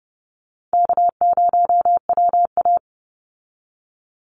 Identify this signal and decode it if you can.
Morse code, heard as a keyed tone, about 30 wpm, 700 Hz. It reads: K0WA
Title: Call Signs Repeated 3x - 30wpm